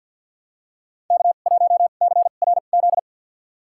Morse code, heard as a keyed tone, about 51 wpm, spaced as for 36 wpm, 700 Hz. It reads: K1CRZ
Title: USA Call Signs - 5 Characters - 2x Character Spacing - 50wpm